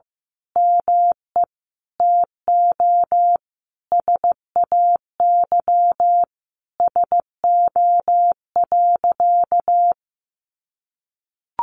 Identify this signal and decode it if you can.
Morse code, heard as a keyed tone, about 15 wpm, 700 Hz. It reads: ME TO SAY SO.
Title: Sentences from Top 700 Words 15wpm